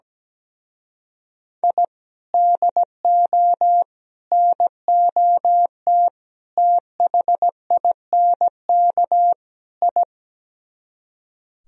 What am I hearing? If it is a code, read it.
Morse code, heard as a keyed tone, about 17 wpm, 700 Hz. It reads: I DO NOT THINK I
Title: Speed-Racing - Sets of 5 Words 17wpm